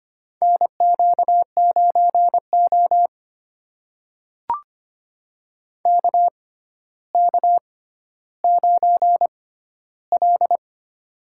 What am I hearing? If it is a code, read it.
Morse code, heard as a keyed tone, about 25 wpm, spaced as for 13 wpm, 700 Hz. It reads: NQ9O KK9L
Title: USA Call Signs - 4 Characters - 6x Character Spacing - 25wpm